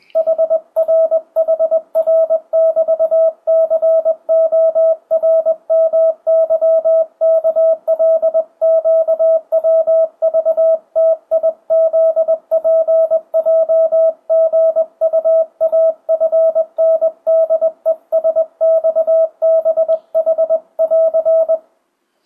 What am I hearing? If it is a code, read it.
Morse code, heard as a keyed tone, about 20 wpm, 650 Hz. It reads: HRHR<BT>CORMYKLQWVTIZPJGUAFNDESXBH<AR>
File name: 2025cw / speedup / 21wpm